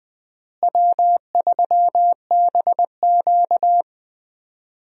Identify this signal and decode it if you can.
Morse code, heard as a keyed tone, about 21 wpm, 700 Hz. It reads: W3BQ
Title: USA Call Signs - 4 Characters - Repeated 2x - 20wpm